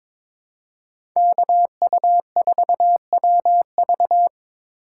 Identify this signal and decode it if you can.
Morse code, heard as a keyed tone, about 21 wpm, 700 Hz. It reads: KU4WV
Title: USA Call Signs - 5 Characters - 4x Character Spacing - 22wpm